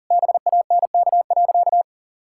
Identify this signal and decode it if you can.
Morse code, heard as a keyed tone, about 37 wpm, 700 Hz. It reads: BANK.
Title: Sentences from Top 1000 Words - Review Part 6 - 40wpm